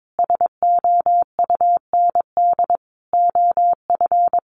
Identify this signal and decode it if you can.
Morse code, heard as a keyed tone, about 22 wpm, 700 Hz. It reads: SOUND OF
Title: A Sound Of Thunder 22wpm